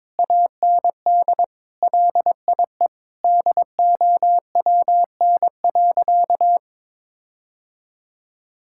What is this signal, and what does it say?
Morse code, heard as a keyed tone, about 21 wpm, 700 Hz. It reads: AND LIE DOWN.
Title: Sentences from Top 1600 Words 22wpm